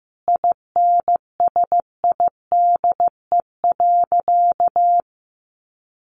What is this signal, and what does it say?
Morse code, heard as a keyed tone, about 15 wpm, 700 Hz. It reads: INSIDE.
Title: Easy Phrases 15wpm